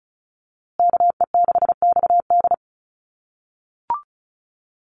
Morse code, heard as a keyed tone, about 34 wpm, 700 Hz. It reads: KE6XD
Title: USA Call Signs Repeated 3x 35wpm